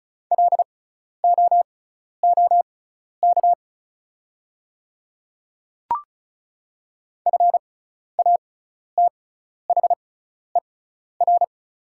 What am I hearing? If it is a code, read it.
Morse code, heard as a keyed tone, about 34 wpm, spaced as for 12 wpm, 690 Hz. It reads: LOOK FATHER
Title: 6x Character Spacing - Top 200 Words - 35wpm